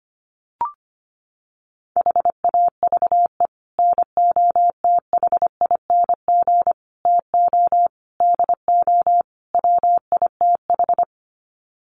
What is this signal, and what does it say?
Morse code, heard as a keyed tone, about 25 wpm, 700 Hz. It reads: HAVE NOTHING TO DO WITH
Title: Sets of 5 Words 25wpm